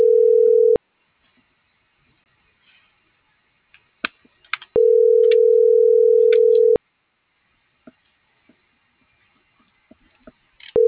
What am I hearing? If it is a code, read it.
Ringback tone (North America), 440 + 480 Hz.